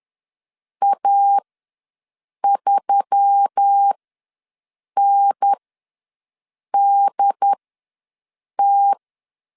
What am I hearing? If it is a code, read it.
Morse code, heard as a keyed tone, about 10 wpm, spaced as for 6 wpm, 800 Hz. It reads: A3NDT
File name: LEZIONI / 5 LBVC / 5 LBV ws 10wpm 3645